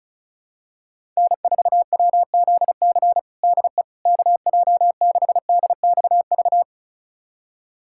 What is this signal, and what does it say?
Morse code, heard as a keyed tone, about 34 wpm, 700 Hz. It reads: N4WZC DE KJ6DXV
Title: USA Callsign DE Callsign 35wpm